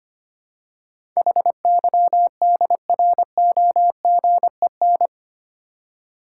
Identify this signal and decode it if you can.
Morse code, heard as a keyed tone, about 25 wpm, 700 Hz. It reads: HYDROGEN
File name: 8 Letter Words - First 3 Letters Given - 25wpm